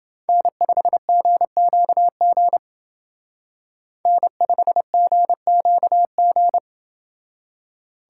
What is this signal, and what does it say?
Morse code, heard as a keyed tone, about 28 wpm, 700 Hz. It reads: N5GQG N5GQG